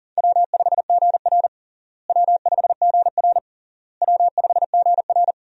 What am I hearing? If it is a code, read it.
Morse code, heard as a keyed tone, about 40 wpm, 700 Hz. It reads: W5GR W5GR W5GR
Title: USA Call Signs - 4 Characters - Repeated 3x - 40wpm